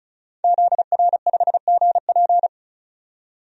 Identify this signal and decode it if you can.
Morse code, heard as a keyed tone, about 36 wpm, 700 Hz. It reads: ZR5GP
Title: DXCC-100 Call Signs - 5 Characters - 35wpm